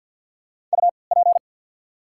Morse code, heard as a keyed tone, about 53 wpm, spaced as for 29 wpm, 690 Hz. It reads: UP
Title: 3x Character Spacing - Top 300 Words - 50wpm